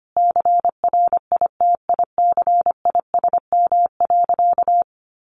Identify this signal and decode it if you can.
Morse code, heard as a keyed tone, about 25 wpm, 700 Hz. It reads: CRITICISM.
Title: Sentences from Top 1300 Words 25wpm